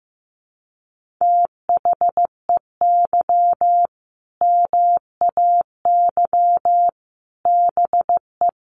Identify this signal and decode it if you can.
Morse code, heard as a keyed tone, about 14 wpm, 700 Hz. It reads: THEY MAY BE
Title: Sentences from Top 1400 Words 15wpm